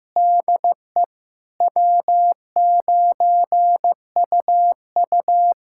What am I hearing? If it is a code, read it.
Morse code, heard as a keyed tone, about 16 wpm, 700 Hz. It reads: DE W9UU
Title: USA Callsign DE Callsign - 4 Characters - 15wpm